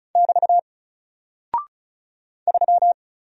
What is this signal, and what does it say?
Morse code, heard as a keyed tone, about 36 wpm, 700 Hz. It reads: <BT> 3
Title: Single Character (Letter:Number:Punctuation:Prosign) 35wpm